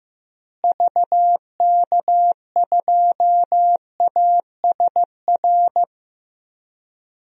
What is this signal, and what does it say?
Morse code, heard as a keyed tone, about 15 wpm, 700 Hz. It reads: VK2ASR
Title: DXCC-100 Call Signs - 6 Characters - Repeated 2x - 15wpm